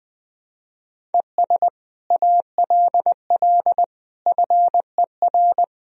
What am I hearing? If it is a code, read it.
Morse code, heard as a keyed tone, about 21 wpm, 700 Hz. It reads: ES ALL FER
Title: Sentences from Top 100 Words and Abbreviations 20wpm